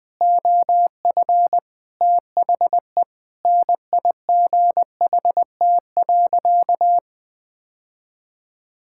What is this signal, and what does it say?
Morse code, heard as a keyed tone, about 20 wpm, 700 Hz. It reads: OF THE NIGHT.
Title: Sentences with 7-Word Phrases - 20wpm